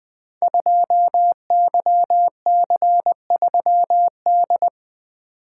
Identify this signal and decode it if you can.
Morse code, heard as a keyed tone, about 20 wpm, 700 Hz. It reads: 2YC3D
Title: Code Groups - 5 Letter:Numbers - 20wpm